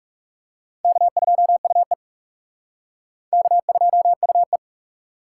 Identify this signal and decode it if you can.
Morse code, heard as a keyed tone, about 38 wpm, 700 Hz. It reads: K2UE K2UE